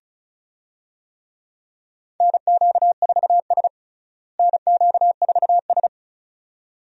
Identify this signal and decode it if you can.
Morse code, heard as a keyed tone, about 34 wpm, 700 Hz. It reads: NQ4S NQ4S
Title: USA Call Signs - 4 Characters - Repeated 2x - 35wpm